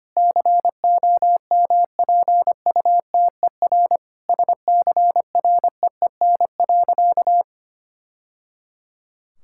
Morse code, heard as a keyed tone, about 25 wpm, 700 Hz. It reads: COMPUTER SCREEN.
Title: Sentences from Top 1000 Words - Review Part 6 - 25wpm